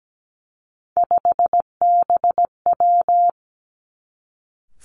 Morse code, heard as a keyed tone, about 17 wpm, 700 Hz. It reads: 5BW
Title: Three Challenging Letter-Number Combinations 17wpm